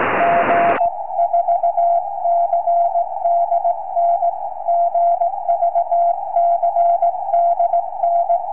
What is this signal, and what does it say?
Morse code, heard as a keyed tone, about 18 wpm, 690 Hz. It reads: G 4CDN GVCDN
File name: tropo / g4cdn